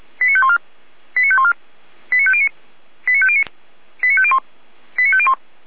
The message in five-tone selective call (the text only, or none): 89514 89514 89509 89509 89501 89501